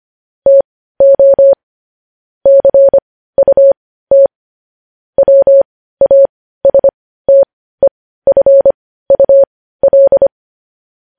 Morse code, heard as a keyed tone, about 25 wpm, spaced as for 15 wpm, 550 Hz. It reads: TO CUT WASTEFUL
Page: news headlines in morse code at 15 wpm